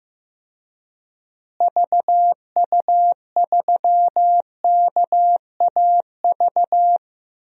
Morse code, heard as a keyed tone, about 16 wpm, 700 Hz. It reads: VU3KAV
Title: DXCC-100 Call Signs - 6 Characters - Repeated 3x - 15wpm